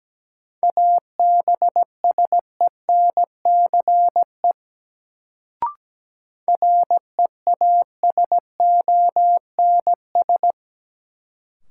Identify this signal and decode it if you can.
Morse code, heard as a keyed tone, about 17 wpm, 700 Hz. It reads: ABSENCE REASONS
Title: Sets of 4 Words - New Word Drill - 17wpm